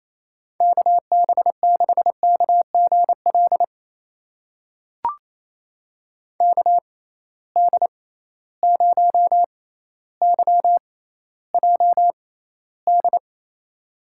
Morse code, heard as a keyed tone, about 27 wpm, spaced as for 14 wpm, 700 Hz. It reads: KB6KGL KD0YJD
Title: USA Call Signs - 6 Characters - 6x Character Spacing - 28wpm